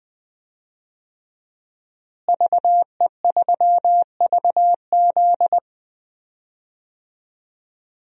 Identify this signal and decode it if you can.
Morse code, heard as a keyed tone, about 20 wpm, 700 Hz. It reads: VE3VZ
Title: DXCC-100 Call Signs - 5 Characters - Repeated 2x - 20wpm